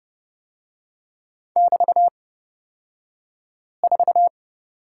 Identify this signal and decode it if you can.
Morse code, heard as a keyed tone, about 31 wpm, spaced as for 5 wpm, 700 Hz. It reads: <BT>4
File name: Lesson 36d - BT - Two Characters 30wpm